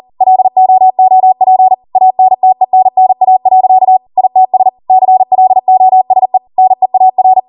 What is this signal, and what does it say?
Morse code, heard as a keyed tone, about 40 wpm, 750 Hz. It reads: LOOP ANTENNA. ITS CLOSE NEAR
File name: Morse / 40WPM